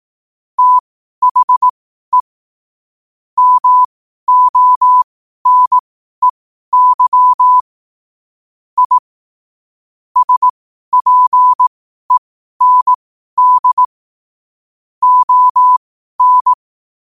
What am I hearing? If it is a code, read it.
Morse code, heard as a keyed tone, about 18 wpm, spaced as for 12 wpm, 1000 Hz. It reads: THE MONEY I SPEND ON
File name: qotdfiles / 12wpm